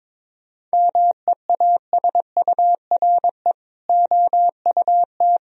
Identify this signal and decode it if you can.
Morse code, heard as a keyed tone, about 21 wpm, 700 Hz. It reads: MEASURE OUT
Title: Sentences from Top 1000 Words - Review Part 2 - 22wpm